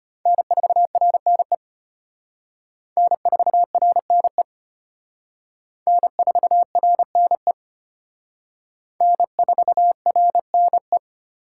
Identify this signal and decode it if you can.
Morse code, heard as a keyed tone, about 31 wpm, 700 Hz. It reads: N4RNE N4RNE N4RNE N4RNE
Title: Speed Racing - USA 1x3 Call Signs 25wpm